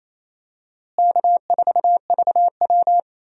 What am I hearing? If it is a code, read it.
Morse code, heard as a keyed tone, about 28 wpm, 700 Hz. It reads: K4VW
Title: USA Call Signs - 4 Characters - Repeated 2x - 28wpm